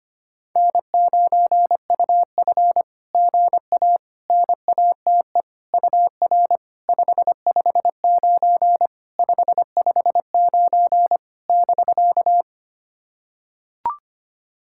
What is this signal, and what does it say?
Morse code, heard as a keyed tone, about 25 wpm, 660 Hz. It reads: N9UF GA NATE UR 559 559 <BK>
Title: SOTA QSO Practice 25wpm